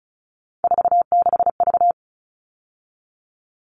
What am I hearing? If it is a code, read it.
Morse code, heard as a keyed tone, about 34 wpm, 700 Hz. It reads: 46V
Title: All Cut Numbers - 3-Digit 35wpm